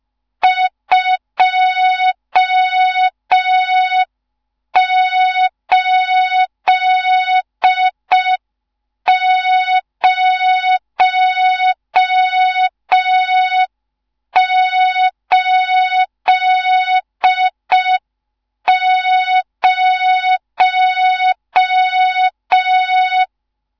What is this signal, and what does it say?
Morse code, heard as a keyed tone, about 5 wpm, 3700 Hz. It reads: 28080
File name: Morze-HA2MN / hang / 14 280 HA2MN